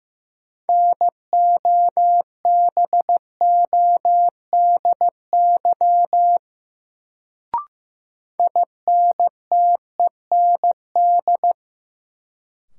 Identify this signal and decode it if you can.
Morse code, heard as a keyed tone, about 15 wpm, 700 Hz. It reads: NOBODY INTEND
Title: Sentences from Top 1600 Words - New Word Drill - 15wpm